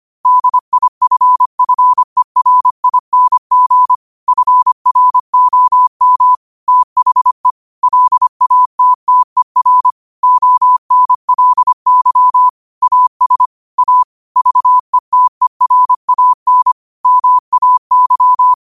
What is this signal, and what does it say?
Morse code, heard as a keyed tone, about 25 wpm, 1000 Hz. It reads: DIFFERING FROM THE LATTER ONLY AS A VETERAN MAY